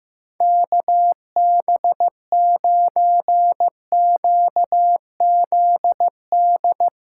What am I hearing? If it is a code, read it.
Morse code, heard as a keyed tone, about 15 wpm, 700 Hz. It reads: KB9QZD
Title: USA Callsign DE Callsign 15wpm